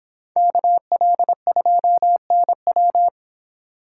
Morse code, heard as a keyed tone, about 24 wpm, 700 Hz. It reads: KL2NW